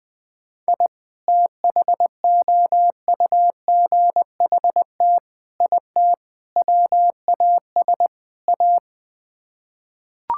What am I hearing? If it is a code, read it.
Morse code, heard as a keyed tone, about 21 wpm, 700 Hz. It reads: I THOUGHT IT WAS A
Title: Build A Phrase - 5-Word Phrases - 20wpm